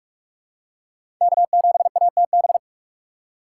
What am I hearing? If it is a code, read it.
Morse code, heard as a keyed tone, about 46 wpm, 700 Hz. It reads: K7ATB
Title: USA Call Signs - 5 Characters - 2x Character Spacing - 45wpm